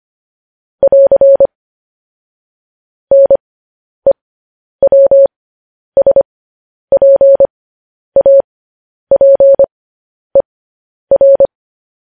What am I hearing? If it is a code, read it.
Morse code, heard as a keyed tone, about 25 wpm, spaced as for 10 wpm, 550 Hz. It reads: <AR> NEWSPAPER
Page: news headlines in morse code at 10 wpm